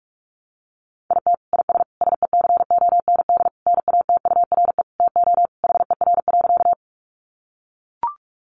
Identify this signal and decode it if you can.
Morse code, heard as a keyed tone, about 44 wpm, 700 Hz. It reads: IT IS SECOND NATURE TO HER.
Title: Sentences from Top 600 Words 45wpm